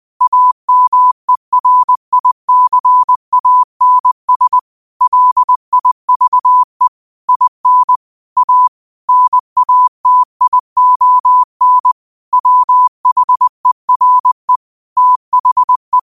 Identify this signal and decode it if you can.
Morse code, heard as a keyed tone, about 20 wpm, 1000 Hz. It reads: AMERICANS LIVE IN A NATION WHERE THE
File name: qotdfiles / 20wpm